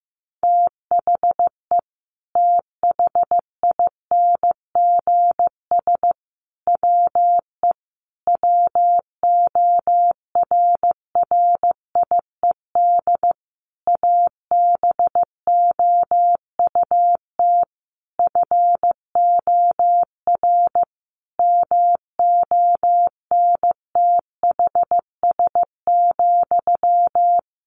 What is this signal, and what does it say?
Morse code, heard as a keyed tone, about 15 wpm, 700 Hz. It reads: THE THINGS WE WORRIED ABOUT FOR MONTHS,
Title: A Sound Of Thunder 15wpm